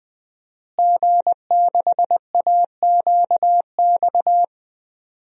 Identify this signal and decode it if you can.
Morse code, heard as a keyed tone, about 20 wpm, 700 Hz. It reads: G6AQX
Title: DXCC-100 Call Signs - 5 Characters - Repeated 3x - 20wpm